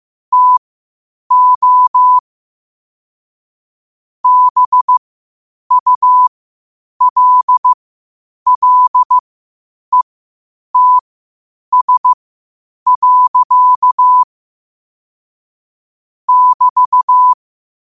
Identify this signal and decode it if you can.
Morse code, heard as a keyed tone, about 15 wpm, spaced as for 8 wpm, 1000 Hz. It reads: TO BULLETS. <BT>
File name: qotdfiles / 8wpm